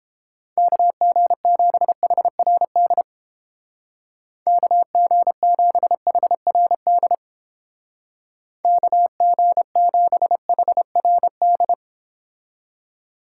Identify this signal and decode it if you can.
Morse code, heard as a keyed tone, about 29 wpm, 700 Hz. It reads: KG7HRD KG7HRD KG7HRD